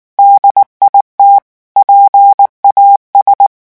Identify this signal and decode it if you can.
Morse code, heard as a keyed tone, about 19 wpm, 800 Hz. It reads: DIT PAS
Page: REVISION 8 - 8 mots minute